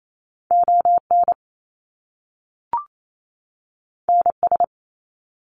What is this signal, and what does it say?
Morse code, heard as a keyed tone, about 28 wpm, 700 Hz. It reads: ON NS
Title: Canadian Provinces and Territories 28wpm